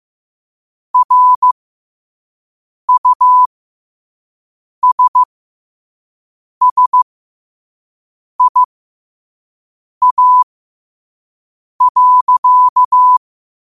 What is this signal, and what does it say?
Morse code, heard as a keyed tone, about 15 wpm, spaced as for 5 wpm, 1000 Hz. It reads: RUSSIA.
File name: qotdfiles / 5wpm